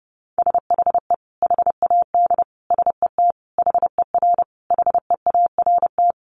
Morse code, heard as a keyed tone, about 30 wpm, 700 Hz. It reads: SHE HAD SET HER HEART
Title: Sentences from Top 1200 Words 30wpm